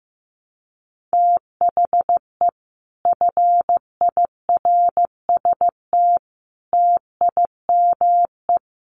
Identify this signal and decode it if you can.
Morse code, heard as a keyed tone, about 15 wpm, 700 Hz. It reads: THE FIRST TIME
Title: Sets of 4 Words 15wpm